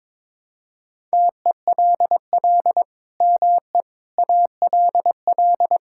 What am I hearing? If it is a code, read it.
Morse code, heard as a keyed tone, about 22 wpm, 700 Hz. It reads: TELL ME ALL
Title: Sentences from Top 1000 Words - Review Part 5 - 22wpm